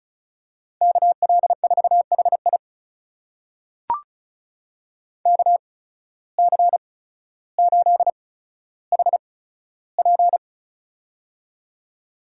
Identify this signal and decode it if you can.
Morse code, heard as a keyed tone, about 34 wpm, spaced as for 15 wpm, 700 Hz. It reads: KL4HI KC8HP
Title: USA Call Signs - 5 Characters - 8x Character Spacing - 35wpm